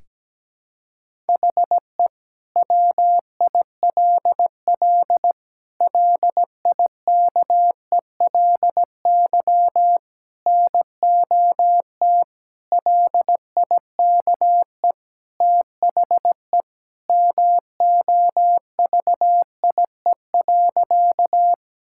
Morse code, heard as a keyed tone, about 17 wpm, 700 Hz. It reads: HE WILL LIKELY NOT LIKE THE MOVIE.
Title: Sentences from Top 1000 Words - Review Part 5 - 17wpm